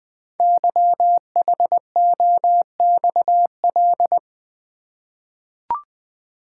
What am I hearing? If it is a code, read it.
Morse code, heard as a keyed tone, about 20 wpm, 700 Hz. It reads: YHOXL
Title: Code Groups - 5 Letters - 20wpm